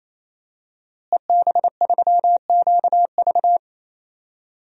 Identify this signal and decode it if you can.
Morse code, heard as a keyed tone, about 28 wpm, 700 Hz. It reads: EB3QV
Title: DXCC-100 Call Signs - 5 Characters - Repeated 3x - 28wpm